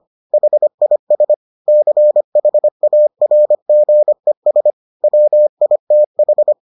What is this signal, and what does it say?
Morse code, heard as a keyed tone, about 25 wpm, 600 Hz. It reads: HIS CHARGES WITH